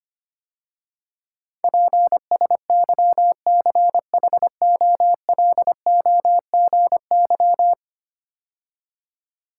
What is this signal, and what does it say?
Morse code, heard as a keyed tone, about 25 wpm, 690 Hz. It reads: PSYCHOLOGY
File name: 10 Letter Words - First 3 Letters Given - 25wpm